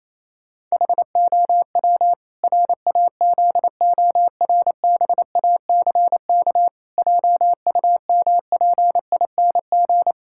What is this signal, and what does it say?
Morse code, heard as a keyed tone, about 28 wpm, 700 Hz. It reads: HOW RAZORBACK JUMPING
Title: Pangram Challenge v2 - 28wpm